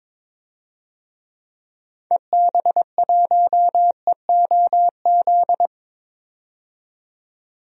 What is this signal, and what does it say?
Morse code, heard as a keyed tone, about 22 wpm, 700 Hz. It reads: EB1EOZ
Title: DXCC-100 Call Signs - 6 Characters - 22wpm